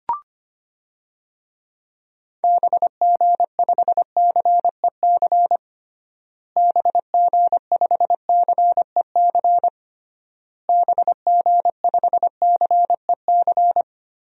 Morse code, heard as a keyed tone, about 25 wpm, 690 Hz. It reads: BG5CEC BG5CEC BG5CEC
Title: DXCC-100 Call Signs - 6 Characters - Repeated 3x - 25wpm